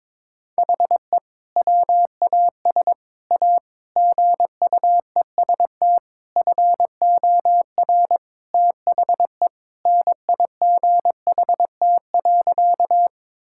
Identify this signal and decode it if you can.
Morse code, heard as a keyed tone, about 22 wpm, 700 Hz. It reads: HE WAS A GUEST FOR THE NIGHT.
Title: Sentences from Top 1600 Words 22wpm